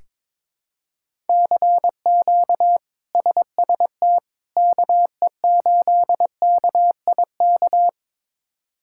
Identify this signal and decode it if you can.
Morse code, heard as a keyed tone, about 22 wpm, 700 Hz. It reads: CQ SST KE8KIK
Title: K1USN SST QSO Practice - 22wpm